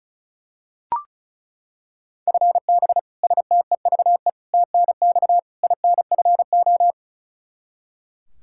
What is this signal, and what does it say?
Morse code, heard as a keyed tone, about 36 wpm, 700 Hz. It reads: FB STEVE TNX INFO
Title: Practice QSO - Exchange 3 - More Info - 35wpm